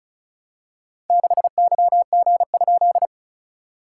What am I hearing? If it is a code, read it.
Morse code, heard as a keyed tone, about 36 wpm, 690 Hz. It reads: 6YG?
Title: SOTA QSO Practice 35wpm